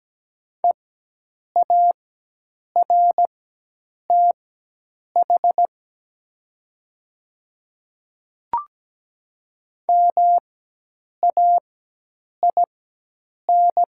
Morse code, heard as a keyed tone, about 17 wpm, spaced as for 8 wpm, 700 Hz. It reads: EARTH MAIN
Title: 4x Character Spacing - Top 300 Words - 17wpm